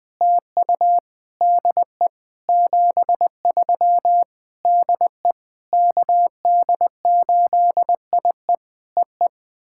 Morse code, heard as a keyed tone, about 20 wpm, 700 Hz. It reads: TU DE 73 DE KD8IE EE
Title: POTA QSO Practice - 20wpm